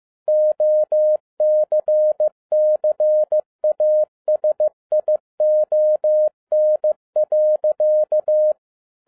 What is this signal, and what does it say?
Morse code, heard as a keyed tone, about 15 wpm, 610 Hz. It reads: OCCASION.
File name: FIELD DAY CONTEST QRV FROM SAKAE VILLAGE 15wpm